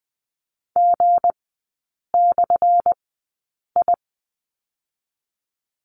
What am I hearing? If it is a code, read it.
Morse code, heard as a keyed tone, about 20 wpm, spaced as for 8 wpm, 700 Hz. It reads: G/I
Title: 205 Lesson 33e - Letter j - Three Characters 20wpm